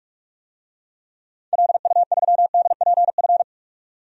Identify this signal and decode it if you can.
Morse code, heard as a keyed tone, about 47 wpm, 700 Hz. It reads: LU3DPF
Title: DXCC-100 Call Signs - 6 Characters - 45wpm